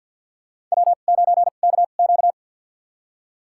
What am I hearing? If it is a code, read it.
Morse code, heard as a keyed tone, about 48 wpm, spaced as for 35 wpm, 700 Hz. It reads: W9KQ
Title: USA Call Signs - 4 Characters - 2x Character Spacing - 50wpm